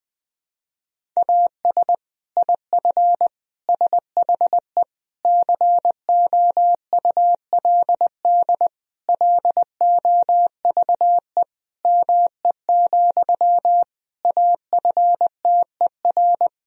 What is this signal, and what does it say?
Morse code, heard as a keyed tone, about 20 wpm, 700 Hz. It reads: AS IF SHE COULD LOVE ME, AFTER
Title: Sentences from Top 1000 Words - Review Part 1 - 20wpm